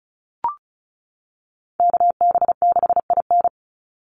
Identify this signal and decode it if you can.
Morse code, heard as a keyed tone, about 34 wpm, 700 Hz. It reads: KB6IN
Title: USA Call Signs 35wpm